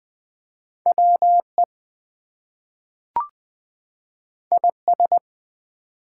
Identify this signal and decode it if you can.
Morse code, heard as a keyed tone, about 20 wpm, 700 Hz. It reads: WE IS
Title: Two Letter Words - 20wpm